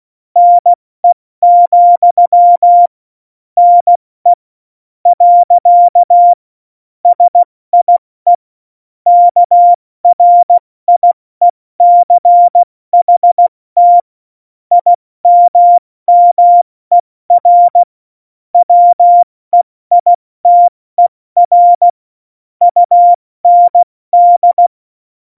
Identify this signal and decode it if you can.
Morse code, heard as a keyed tone, about 16 wpm, spaced as for 14 wpm, 700 Hz. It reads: NE, NE. SIE KRIECHT IMMER WEITER UND